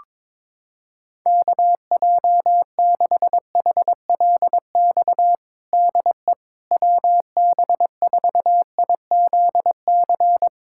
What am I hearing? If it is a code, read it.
Morse code, heard as a keyed tone, about 21 wpm, 700 Hz. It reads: KJ6HLX DE WB4IZC
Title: USA Callsign DE Callsign - 6 Characters - 22wpm